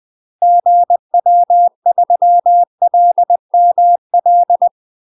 Morse code, heard as a keyed tone, about 20 wpm, 700 Hz. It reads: GW3LML
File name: UK MASTER QSOs TRAINING / 20WPM / UK MASTER QSOs 0071